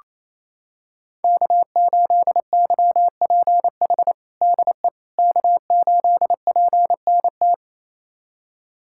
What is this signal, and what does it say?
Morse code, heard as a keyed tone, about 27 wpm, 700 Hz. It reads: K8YPH DE K8PNT
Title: USA Callsign DE Callsign - 5 Characters - 28wpm